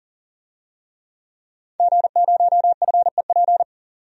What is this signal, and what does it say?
Morse code, heard as a keyed tone, about 40 wpm, 700 Hz. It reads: G0FEP
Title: DXCC-100 Call Signs - 5 Characters - 40wpm